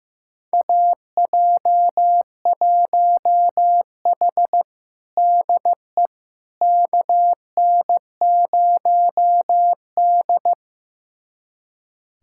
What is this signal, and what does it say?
Morse code, heard as a keyed tone, about 15 wpm, 700 Hz. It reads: AJ1H DE KN0D
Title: USA Callsign DE Callsign - 4 Characters - 15wpm